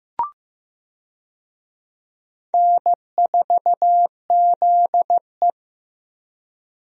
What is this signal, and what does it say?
Morse code, heard as a keyed tone, about 15 wpm, 700 Hz. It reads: N4ZE
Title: USA Call Signs - 4 Characters - Repeated 3x - 15wpm